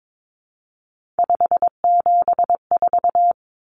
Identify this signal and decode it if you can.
Morse code, heard as a keyed tone, about 22 wpm, 700 Hz. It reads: 574
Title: Numbers - 3-Digit 22wpm